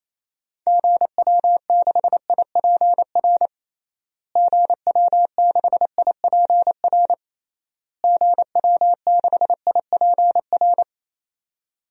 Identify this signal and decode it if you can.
Morse code, heard as a keyed tone, about 28 wpm, 700 Hz. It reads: GW6IPR GW6IPR GW6IPR
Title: DXCC-100 Call Signs - 6 Characters - Repeated 3x - 28wpm